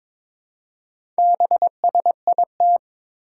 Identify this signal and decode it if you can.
Morse code, heard as a keyed tone, about 22 wpm, 700 Hz. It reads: BSIT
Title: Four Challenging Letter-Number Combinations 22wpm